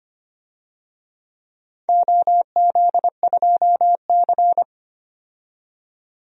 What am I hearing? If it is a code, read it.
Morse code, heard as a keyed tone, about 25 wpm, 700 Hz. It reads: OZ2C
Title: DXCC-100 Call Signs - 4 Characters - 25wpm